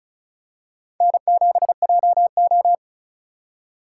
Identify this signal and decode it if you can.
Morse code, heard as a keyed tone, about 36 wpm, 700 Hz. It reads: N7JO
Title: USA Call Signs - 4 Characters - 4x Character Spacing - 35wpm